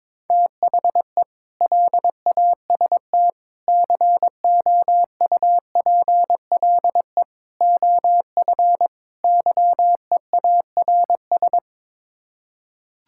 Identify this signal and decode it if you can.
Morse code, heard as a keyed tone, about 22 wpm, 700 Hz. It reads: THE LAST COUPLE OF YEARS
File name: practice / 22wpm / Build A Phrase - 5-Word Phrases - 22wpm